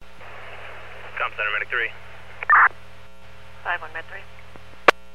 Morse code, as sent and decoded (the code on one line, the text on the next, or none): none